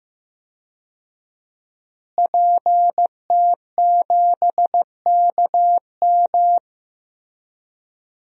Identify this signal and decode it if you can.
Morse code, heard as a keyed tone, about 15 wpm, 700 Hz. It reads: PT7KM
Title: DXCC-100 Call Signs - 5 Characters - Repeated 2x - 15wpm